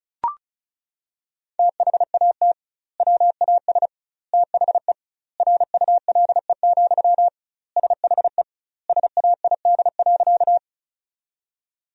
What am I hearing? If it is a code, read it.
Morse code, heard as a keyed tone, about 36 wpm, 700 Hz. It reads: THAT WAS THE RULE, SHE SAID.
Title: 2x Word-Spacing - Sentences from Top 400 Words - 35wpm